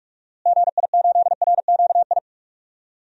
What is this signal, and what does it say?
Morse code, heard as a keyed tone, about 46 wpm, 700 Hz. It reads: GI8RQI
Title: DXCC-100 Call Signs - 6 Characters - Repeated 2x - 45wpm